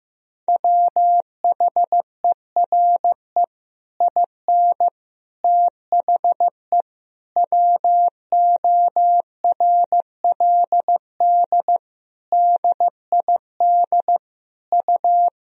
Morse code, heard as a keyed tone, about 15 wpm, 700 Hz. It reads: WHERE IN THE WORLD DID U
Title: Sentences from Top 200 Words and Abbreviations 15wpm